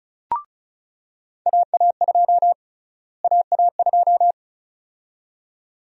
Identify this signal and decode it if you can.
Morse code, heard as a keyed tone, about 36 wpm, 700 Hz. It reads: AA2 AA2
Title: USA Call Sign Prefixes - Repeated 2x - 35wpm